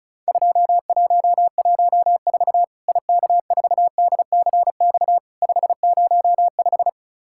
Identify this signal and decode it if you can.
Morse code, heard as a keyed tone, about 36 wpm, 700 Hz. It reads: 2114 IK4DCX 505
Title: Worked All Europe DX Contest - QTC Exchange - 35wpm